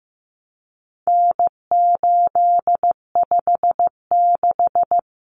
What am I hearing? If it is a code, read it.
Morse code, heard as a keyed tone, about 15 wpm, 700 Hz. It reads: N856
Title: Common Cut Numbers - 4-Digit 15wpm